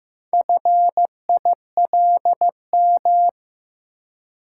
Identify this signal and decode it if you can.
Morse code, heard as a keyed tone, about 15 wpm, 700 Hz. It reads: FILM
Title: Four Letter Words - 15wpm